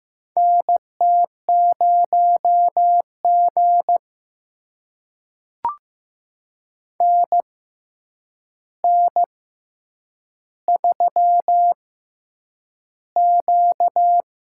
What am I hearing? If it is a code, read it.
Morse code, heard as a keyed tone, about 15 wpm, spaced as for 8 wpm, 700 Hz. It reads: NT0G NN3Q
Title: USA Call Signs - 4 Characters - 6x Character Spacing - 15wpm